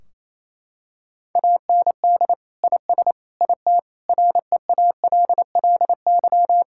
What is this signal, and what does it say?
Morse code, heard as a keyed tone, about 28 wpm, 700 Hz. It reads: AND IS IT REALLY
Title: Sentences from Top 1600 Words 28wpm